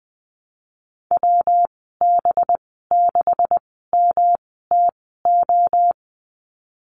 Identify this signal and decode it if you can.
Morse code, heard as a keyed tone, about 20 wpm, spaced as for 14 wpm, 700 Hz. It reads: WB6MTO
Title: 193 Lesson 31g - Letter v - Call Signs 20wpm